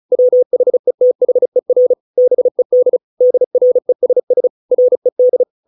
Morse code, heard as a keyed tone, about 34 wpm, 500 Hz. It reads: WHETHER BED DRESS RED